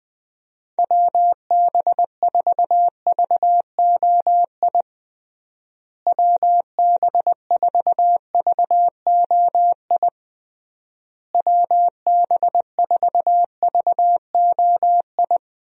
Morse code, heard as a keyed tone, about 20 wpm, 700 Hz. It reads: WB4VOI WB4VOI WB4VOI
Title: USA Call Signs - 6 Characters - Repeated 3x - 20wpm